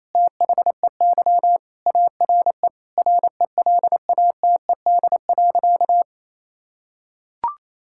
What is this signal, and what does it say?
Morse code, heard as a keyed tone, about 28 wpm, 700 Hz. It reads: THEY ARE RELATED.
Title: Sentences from Top 1000 Words - Review Part 5 - 28wpm